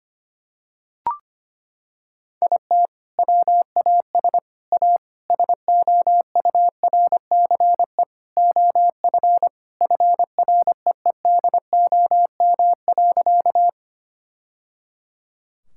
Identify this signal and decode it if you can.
Morse code, heard as a keyed tone, about 25 wpm, 700 Hz. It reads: IT WAS A SOURCE OF FREEDOM.
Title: Sentences from Top 1000 Words - Review Part 5 - 25wpm